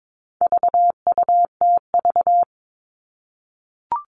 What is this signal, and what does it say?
Morse code, heard as a keyed tone, about 22 wpm, 700 Hz. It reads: VUTV
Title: All Cut Numbers - 4-Digit 22wpm